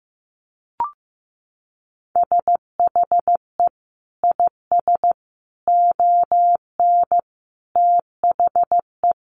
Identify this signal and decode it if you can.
Morse code, heard as a keyed tone, about 15 wpm, 700 Hz. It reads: SHE IS ON THE
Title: Sentences from Top 1500 Words 15wpm